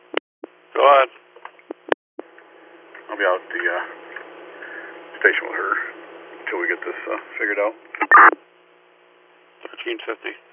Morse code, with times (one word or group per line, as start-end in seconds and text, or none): none